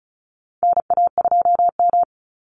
Morse code, heard as a keyed tone, about 34 wpm, 700 Hz. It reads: NA2M
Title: Call Signs - 35wpm